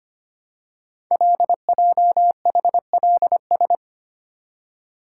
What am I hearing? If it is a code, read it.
Morse code, heard as a keyed tone, about 25 wpm, 700 Hz. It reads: LJHLS
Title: Code Groups - 5 Letters - 25wpm